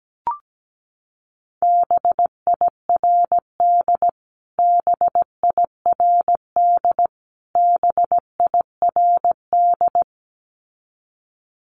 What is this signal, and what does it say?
Morse code, heard as a keyed tone, about 17 wpm, 700 Hz. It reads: BIRD BIRD BIRD
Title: Top 500 Words Repeated 17wpm